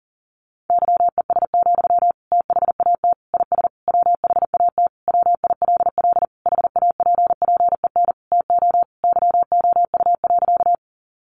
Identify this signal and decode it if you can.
Morse code, heard as a keyed tone, about 40 wpm, 700 Hz. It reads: YES, THAT IS WHAT WILL HAPPEN TO YOU.
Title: Sentences from Top 500 Words 40wpm